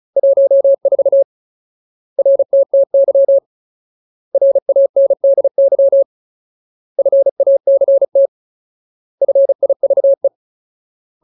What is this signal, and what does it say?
Morse code, heard as a keyed tone, about 34 wpm, 550 Hz. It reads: 14 RTTY RANDY FACT FIVE
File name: Ultimate Word List / 35WPM Ultimate